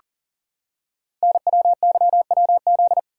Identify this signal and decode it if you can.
Morse code, heard as a keyed tone, about 43 wpm, 700 Hz. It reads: NWYWZ
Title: Code Groups - 5 Letters - 40wpm